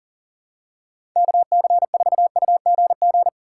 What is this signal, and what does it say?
Morse code, heard as a keyed tone, about 38 wpm, 700 Hz. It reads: KC4UGG
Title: USA Call Signs - 6 Characters - 6x Character Spacing - 40wpm